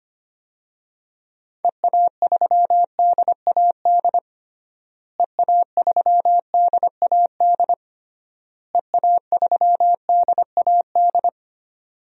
Morse code, heard as a keyed tone, about 25 wpm, 700 Hz. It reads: EA3DAD EA3DAD EA3DAD
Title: DXCC-100 Call Signs - 6 Characters - Repeated 3x - 25wpm